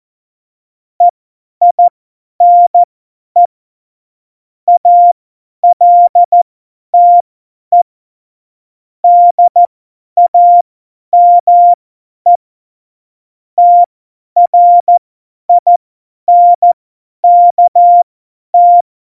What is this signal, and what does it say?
Morse code, heard as a keyed tone, about 14 wpm, spaced as for 10 wpm, 700 Hz. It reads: EINE ALTE DAME TRINKT
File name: Morsekurs / Witze / Witz 11. Februar 2020, 50 F70